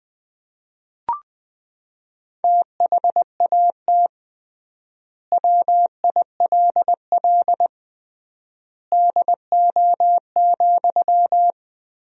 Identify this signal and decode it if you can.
Morse code, heard as a keyed tone, about 21 wpm, 700 Hz. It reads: THAT WILL DO,
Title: 3x Word-Spacing - Sentences from Top 100 Words - 20wpm